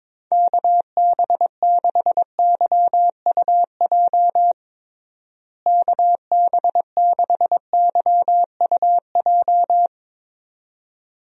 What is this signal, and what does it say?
Morse code, heard as a keyed tone, about 22 wpm, 700 Hz. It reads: KB6YUJ KB6YUJ
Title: USA Call Signs - 6 Characters - Repeated 2x - 22wpm